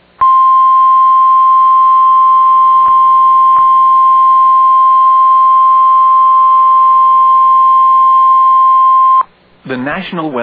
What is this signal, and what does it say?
Emergency Alert System (SAME) tones (the 1050 Hz attention tone), with speech. It is heard from 0.3 to 9.2 s.